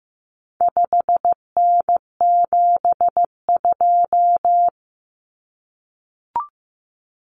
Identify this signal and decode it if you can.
Morse code, heard as a keyed tone, about 15 wpm, 700 Hz. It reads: 5N72
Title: Common Cut Numbers - 4-Digit 15wpm